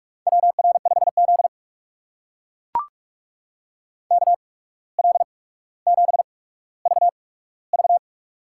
Morse code, heard as a keyed tone, about 44 wpm, spaced as for 18 wpm, 700 Hz. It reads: WR5Z KL7VV
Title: USA Call Signs - 8x Character Spacing - 45wpm